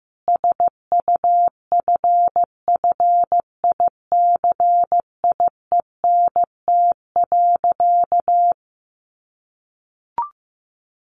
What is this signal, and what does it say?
Morse code, heard as a keyed tone, about 15 wpm, 700 Hz. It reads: SUFFICIENT.
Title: Sentences from Top 1400 Words 15wpm